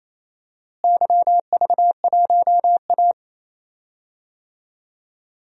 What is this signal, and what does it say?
Morse code, heard as a keyed tone, about 27 wpm, 700 Hz. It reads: YV1A
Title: DXCC-100 Call Signs - 4 Characters - Repeated 3x - 28wpm